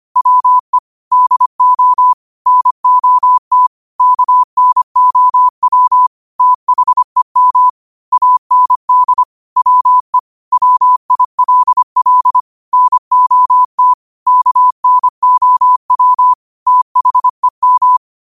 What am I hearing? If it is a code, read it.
Morse code, heard as a keyed tone, about 25 wpm, 1000 Hz. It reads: WE DO NOT KNOW THEM AND WE WILL NOT KNOW THEM
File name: qotdfiles / 25wpm